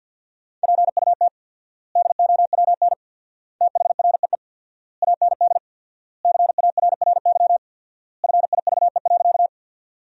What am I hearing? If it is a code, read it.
Morse code, heard as a keyed tone, about 50 wpm, 700 Hz. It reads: PUT DOWN THREE AND CARRY FIVE.
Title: 4x Word-Spacing - Sentences from Top 400 Words - 50wpm